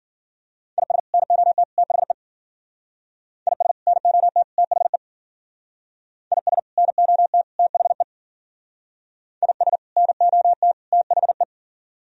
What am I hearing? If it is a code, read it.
Morse code, heard as a keyed tone, about 50 wpm, 700 Hz. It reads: IS NOT THE IS NOT THE IS NOT THE IS NOT THE